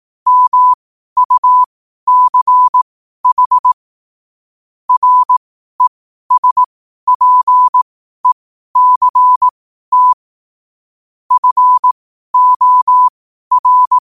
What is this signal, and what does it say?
Morse code, heard as a keyed tone, about 18 wpm, spaced as for 12 wpm, 1000 Hz. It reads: MUCH RESPECT FOR
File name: qotdfiles / 12wpm